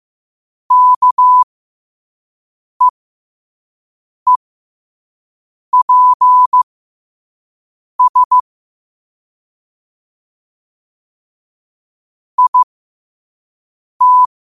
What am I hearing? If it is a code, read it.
Morse code, heard as a keyed tone, about 15 wpm, spaced as for 5 wpm, 1000 Hz. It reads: KEEPS IT
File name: qotdfiles / 5wpm